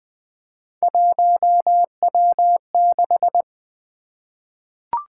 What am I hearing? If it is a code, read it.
Morse code, heard as a keyed tone, about 20 wpm, 700 Hz. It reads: 1W6
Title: Three Letter-Number Combinations - 20wpm